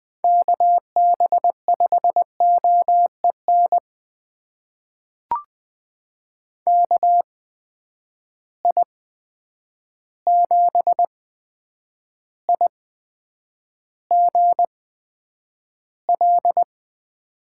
Morse code, heard as a keyed tone, about 20 wpm, spaced as for 9 wpm, 700 Hz. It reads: KB5OEN KI7IGL
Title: USA Call Signs - 6 Characters - 8x Character Spacing - 20wpm